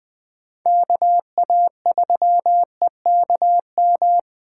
Morse code, heard as a keyed tone, about 20 wpm, 700 Hz. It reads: KA3EKM
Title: USA Call Signs - 8x Character Spacing - 20wpm